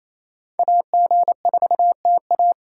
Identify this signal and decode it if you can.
Morse code, heard as a keyed tone, about 37 wpm, 700 Hz. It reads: AG4TA